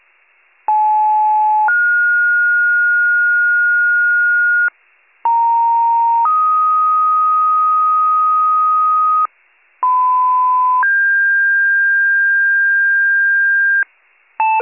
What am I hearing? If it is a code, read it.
Two-tone page, heard as a one-second tone, then a three-second tone, about 860 then 1430 Hz, about 930 then 1250 Hz, about 1010 then 1600 Hz, with noise. It is heard from 0.7 to 13.8 s.